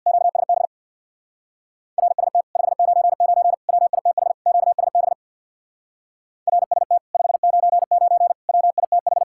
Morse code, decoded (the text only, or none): QSB RST 599 WITH QSB RST 599 WITH